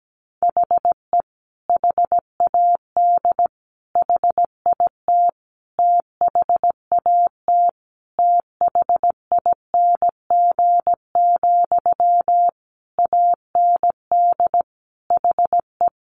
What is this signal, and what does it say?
Morse code, heard as a keyed tone, about 17 wpm, 700 Hz. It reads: HE HAD HIT THAT THING, AND HE
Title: Sentences from Top 1200 Words 17wpm